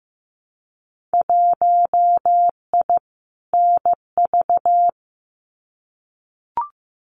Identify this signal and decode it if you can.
Morse code, heard as a keyed tone, about 15 wpm, 700 Hz. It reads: 1I NV
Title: Winter Field Day Exchanges Encore 15wpm